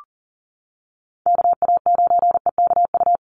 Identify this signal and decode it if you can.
Morse code, heard as a keyed tone, about 40 wpm, 700 Hz. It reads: KA9EKU
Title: USA Call Signs Repeated 2x 40wpm